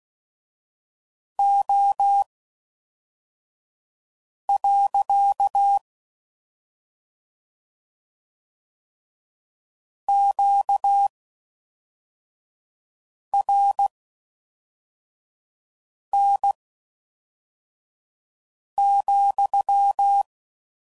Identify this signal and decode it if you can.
Morse code, heard as a keyed tone, about 16 wpm, spaced as for 4 wpm, 800 Hz. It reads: O. QRN,